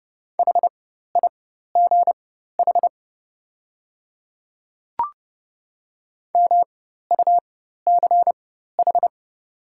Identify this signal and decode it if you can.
Morse code, heard as a keyed tone, about 30 wpm, spaced as for 14 wpm, 700 Hz. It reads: HIGH MUCH
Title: 4x Character Spacing - Top 300 Words - 30wpm